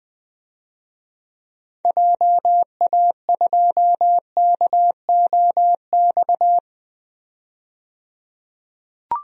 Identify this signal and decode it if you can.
Morse code, heard as a keyed tone, about 20 wpm, 700 Hz. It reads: JA2KOX
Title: DXCC-100 Call Signs - 6 Characters - 20wpm